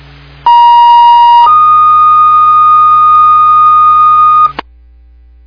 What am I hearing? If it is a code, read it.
Two-tone page, heard as a one-second tone, then a three-second tone, about 950 then 1230 Hz, with noise. It is heard from 0.5 to 4.5 s.